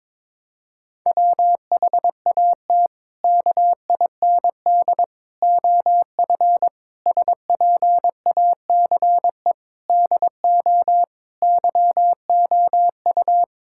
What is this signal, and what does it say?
Morse code, heard as a keyed tone, about 22 wpm, 700 Hz. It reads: WHAT KIND OF SPACE DO YOU
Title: Sentences from Top 1000 Words - Review Part 5 - 22wpm